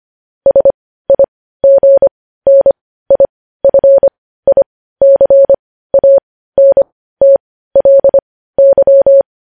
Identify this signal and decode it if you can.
Morse code, heard as a keyed tone, about 25 wpm, spaced as for 15 wpm, 550 Hz. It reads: SIGNIFICANTLY
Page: News Headlines in Morse Code at 15 WPM